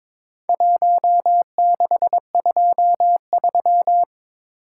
Morse code, heard as a keyed tone, about 22 wpm, 700 Hz. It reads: 1623
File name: practice / 22wpm / Four Letter-Number Combinations - 22wpm